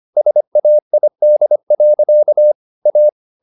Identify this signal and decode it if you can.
Morse code, heard as a keyed tone, about 25 wpm, 600 Hz. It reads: SAID. A